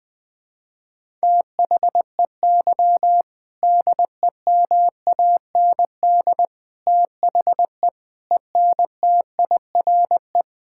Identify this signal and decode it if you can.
Morse code, heard as a keyed tone, about 20 wpm, 700 Hz. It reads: THEY DEMAND THE ENTIRE
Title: Sentences from Top 1000 Words - Review Part 5 - 20wpm